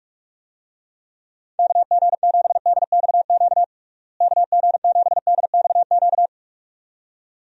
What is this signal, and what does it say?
Morse code, heard as a keyed tone, about 41 wpm, 700 Hz. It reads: KG7DXQ KG7DXQ
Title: USA Call Signs - 6 Characters - Repeated 2x - 45wpm